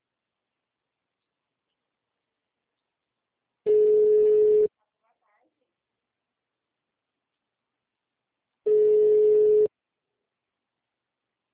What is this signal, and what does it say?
Ringback tone (Europe, 425 Hz), with speech.
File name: out- -2051- - - .18860